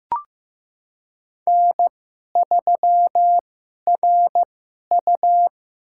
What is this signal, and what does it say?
Morse code, heard as a keyed tone, about 15 wpm, spaced as for 11 wpm, 700 Hz. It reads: N3RU
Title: USA Call Signs - 4 Characters - 2x Character Spacing - 15wpm